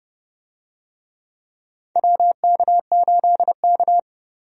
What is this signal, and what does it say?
Morse code, heard as a keyed tone, about 30 wpm, 700 Hz. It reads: WK8K
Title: USA Call Signs - 4 Characters - Repeated 2x - 30wpm